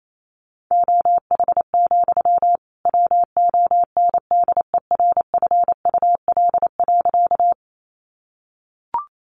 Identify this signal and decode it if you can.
Morse code, heard as a keyed tone, about 28 wpm, 700 Hz. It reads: OH, WONDERFUL.
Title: Sentences from Top 1100 Words 28wpm